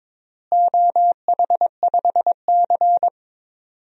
Morse code, heard as a keyed tone, about 22 wpm, 700 Hz. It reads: OH5C
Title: DXCC-100 Call Signs - 4 Characters - Repeated 3x - 22wpm